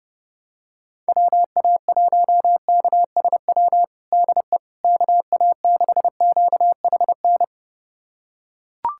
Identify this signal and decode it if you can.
Morse code, heard as a keyed tone, about 32 wpm, 700 Hz. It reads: WA1KSW DE KA6QHN
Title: USA Callsign DE Callsign - 6 Characters - 30wpm